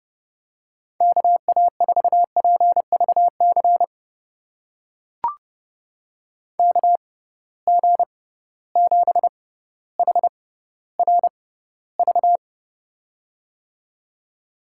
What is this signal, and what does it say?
Morse code, heard as a keyed tone, about 30 wpm, spaced as for 15 wpm, 700 Hz. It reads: KA4PVC KG7HRV
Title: USA Call Signs - 6 Characters - 6x Character Spacing - 30wpm